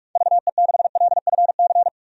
Morse code, heard as a keyed tone, about 46 wpm, 700 Hz. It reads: VE6LFC
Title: DXCC-100 Call Signs - 6 Characters - 45wpm